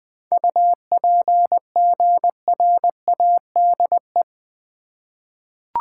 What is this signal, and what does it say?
Morse code, heard as a keyed tone, about 21 wpm, 700 Hz. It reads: UPGRADE
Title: 7 Letter Words - First 3 Letters Given - 20wpm